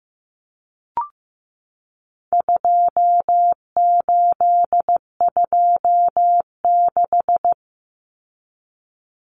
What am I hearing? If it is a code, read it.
Morse code, heard as a keyed tone, about 15 wpm, 700 Hz. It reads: 2826
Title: Numbers - 4-Digit 15wpm